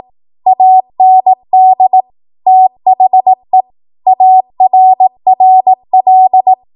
Morse code, heard as a keyed tone, about 18 wpm, 750 Hz. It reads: AND THE ARRL
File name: Archive / 18 WPM / 18WPM